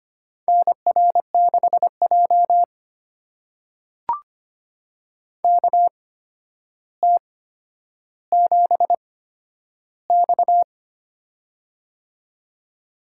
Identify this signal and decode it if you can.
Morse code, heard as a keyed tone, about 25 wpm, spaced as for 11 wpm, 700 Hz. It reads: NR6J KT7X
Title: USA Call Signs - 4 Characters - 8x Character Spacing - 25wpm